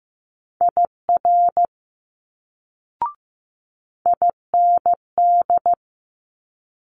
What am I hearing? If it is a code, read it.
Morse code, heard as a keyed tone, about 15 wpm, 700 Hz. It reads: IR IND
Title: Multi-letter Phonemes Quick Fire 15wpm